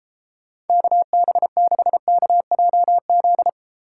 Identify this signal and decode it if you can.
Morse code, heard as a keyed tone, about 33 wpm, 700 Hz. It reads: KB6KJZ